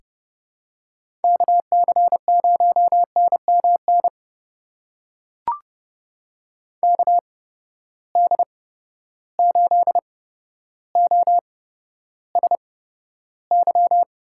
Morse code, heard as a keyed tone, about 29 wpm, spaced as for 13 wpm, 700 Hz. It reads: KC0NMN KD8OSY
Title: USA Call Signs - 6 Characters - 8x Character Spacing - 30wpm